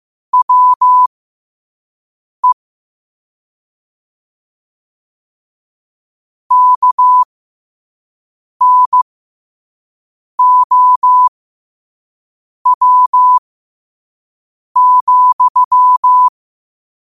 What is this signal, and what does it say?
Morse code, heard as a keyed tone, about 15 wpm, spaced as for 5 wpm, 1000 Hz. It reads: WE KNOW,
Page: New quotes every day in morse code at 5 Words per minute.